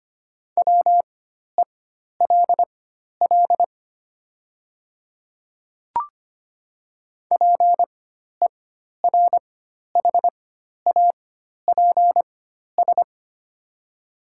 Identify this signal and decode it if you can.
Morse code, heard as a keyed tone, about 25 wpm, spaced as for 12 wpm, 700 Hz. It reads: WELL PERHAPS
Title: 4x Character Spacing - Top 500 Words - 25wpm